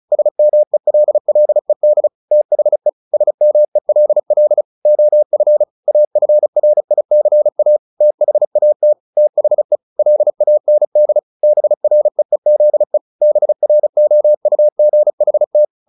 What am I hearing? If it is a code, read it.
Morse code, heard as a keyed tone, about 35 wpm, 600 Hz. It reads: SMELLED THE SMELL OF AFRICA THAT THE LAND BREEZE BROUGHT
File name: the old man and the sea / the old man and the sea 1 35wpm